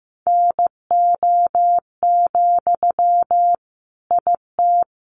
Morse code, heard as a keyed tone, about 15 wpm, 700 Hz. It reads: NO, IT
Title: Sentences from Top 1000 Words 15wpm